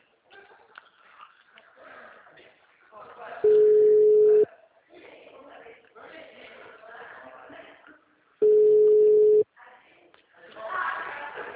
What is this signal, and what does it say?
Ringback tone (Europe, 425 Hz), with speech.